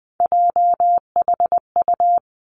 Morse code, heard as a keyed tone, about 20 wpm, 700 Hz. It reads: JHU
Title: Three Challenging Letter-Number Combinations 20wpm